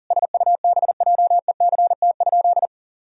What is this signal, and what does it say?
Morse code, heard as a keyed tone, about 40 wpm, 700 Hz. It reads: SUBJECT?
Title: Sentences from Top 1000 Words - Review Part 1 - 40wpm